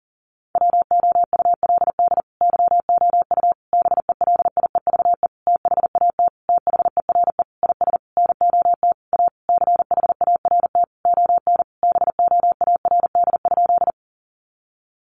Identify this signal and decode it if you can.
Morse code, heard as a keyed tone, about 40 wpm, 700 Hz. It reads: WOULD YOU BELIEVE THAT THERE IS NOT A CHART ON BOARD?
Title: Sentences from Top 1300 Words 40wpm